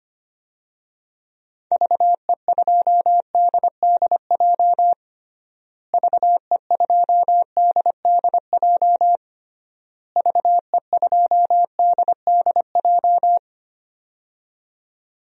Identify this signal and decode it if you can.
Morse code, heard as a keyed tone, about 25 wpm, 700 Hz. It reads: VE2DDJ VE2DDJ VE2DDJ
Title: DXCC-100 Call Signs - 6 Characters - Repeated 3x - 25wpm